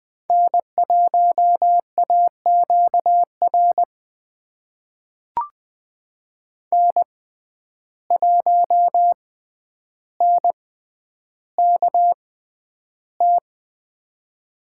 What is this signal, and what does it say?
Morse code, heard as a keyed tone, about 20 wpm, spaced as for 10 wpm, 700 Hz. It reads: N1AQR N1NKT
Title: USA Call Signs - 5 Characters - 6x Character Spacing - 20wpm